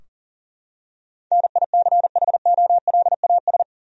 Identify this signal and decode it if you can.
Morse code, heard as a keyed tone, about 40 wpm, 700 Hz. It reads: NICHOLAS
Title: USA Common Names - 40wpm